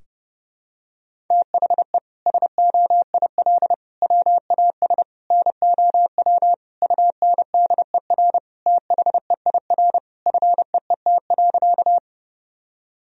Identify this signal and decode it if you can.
Morse code, heard as a keyed tone, about 30 wpm, 700 Hz. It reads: THE SOIL WAS NOW UNDER THEIR FEET.
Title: Sentences from Top 1000 Words - Review Part 4 - 30wpm